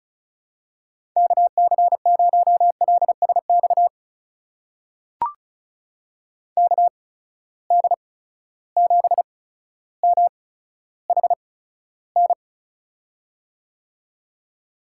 Morse code, heard as a keyed tone, about 36 wpm, spaced as for 15 wpm, 700 Hz. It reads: KC0LSX KD7MHN
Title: USA Call Signs - 6 Characters - 8x Character Spacing - 35wpm